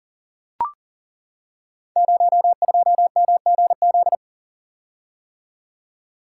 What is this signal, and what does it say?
Morse code, heard as a keyed tone, about 39 wpm, 700 Hz. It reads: 02MGZ
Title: Code Groups - 5 Letter:Numbers - 40wpm